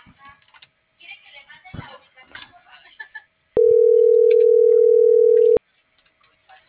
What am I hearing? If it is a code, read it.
Ringback tone (North America), 440 + 480 Hz.
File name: CENTRAL MAYAB / out lizbeth.mayab